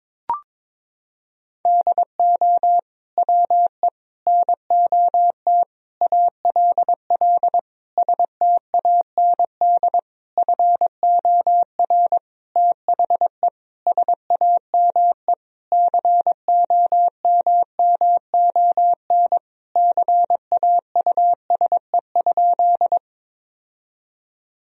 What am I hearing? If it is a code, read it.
Morse code, heard as a keyed tone, about 22 wpm, 700 Hz. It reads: DO WE NOT ALL STAND FOR THE SAME COMMON CAUSE?
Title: Sentences from Top 1000 Words - Review Part 3 - 22wpm